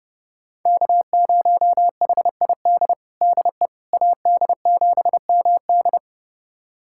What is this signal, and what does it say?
Morse code, heard as a keyed tone, about 30 wpm, 700 Hz. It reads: K0HID DE AD7MD
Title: USA Callsign DE Callsign - 5 Characters - 30wpm